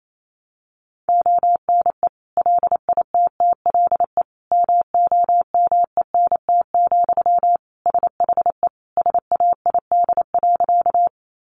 Morse code, heard as a keyed tone, about 28 wpm, 700 Hz. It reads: ONE LITTLE MOMENT, SHE SAID.
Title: Sentences from Top 1000 Words 28wpm